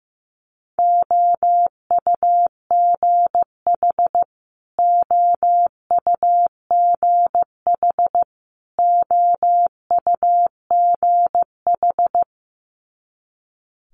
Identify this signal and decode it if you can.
Morse code, heard as a keyed tone, about 15 wpm, 700 Hz. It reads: OUGH OUGH OUGH
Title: Multi-letter Phonemes Repeated 3x 15wpm